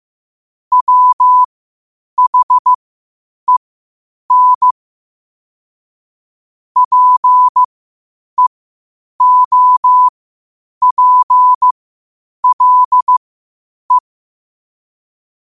Morse code, heard as a keyed tone, about 15 wpm, spaced as for 8 wpm, 1000 Hz. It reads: WHEN PEOPLE
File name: qotdfiles / 8wpm